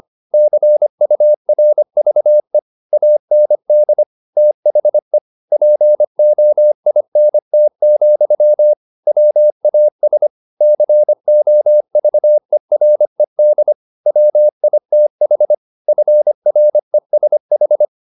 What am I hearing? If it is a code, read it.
Morse code, heard as a keyed tone, about 25 wpm, 600 Hz. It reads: CURVE AND THE POINT, WAS COVERED WITH FRESH
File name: the old man and the sea part 1 25wpm